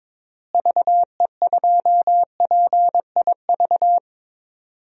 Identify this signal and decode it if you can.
Morse code, heard as a keyed tone, about 22 wpm, 700 Hz. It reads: VE2PIV
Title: DXCC-100 Call Signs - 6 Characters - Repeated 3x - 22wpm